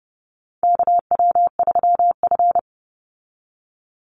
Morse code, heard as a keyed tone, about 30 wpm, 700 Hz. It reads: KW3F
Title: Call Signs - 30wpm